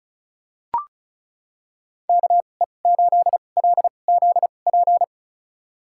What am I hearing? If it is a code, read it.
Morse code, heard as a keyed tone, about 36 wpm, spaced as for 25 wpm, 700 Hz. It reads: KE8LZP
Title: USA Call Signs - 6 Characters - 2x Character Spacing - 35wpm